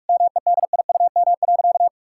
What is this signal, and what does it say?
Morse code, heard as a keyed tone, about 41 wpm, 700 Hz. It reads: MEDIUM.
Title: Sentences from Top 1000 Words - Review Part 6 - 45wpm